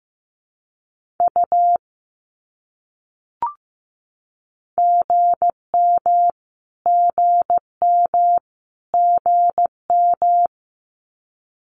Morse code, heard as a keyed tone, about 15 wpm, 700 Hz. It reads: U GM GM GM
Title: Abbreviations Repeated 3x - 15wpm